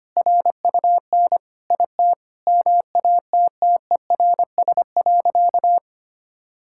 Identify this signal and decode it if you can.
Morse code, heard as a keyed tone, about 25 wpm, 700 Hz. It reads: RUN IT MATTERS.
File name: practice / 25wpm / Sentences with 4-Word Phrases - 25wpm